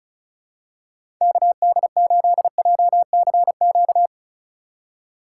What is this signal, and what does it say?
Morse code, heard as a keyed tone, about 36 wpm, 700 Hz. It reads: KD8JCQ
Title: USA Call Signs - 6 Characters - 3x Character Spacing - 35wpm